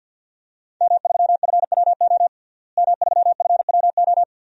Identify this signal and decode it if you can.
Morse code, heard as a keyed tone, about 51 wpm, 700 Hz. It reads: M3FWO M3FWO
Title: DXCC-100 Call Signs - 5 Characters - Repeated 3x - 50wpm